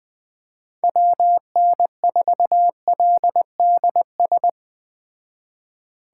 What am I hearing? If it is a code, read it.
Morse code, heard as a keyed tone, about 21 wpm, 700 Hz. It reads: WN4LDS
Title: Speed Racing - USA Call Signs 17wpm